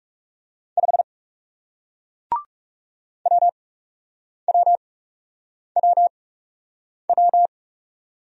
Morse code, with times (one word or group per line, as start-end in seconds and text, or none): 0.8-1.0 5
3.3-7.5 WWWW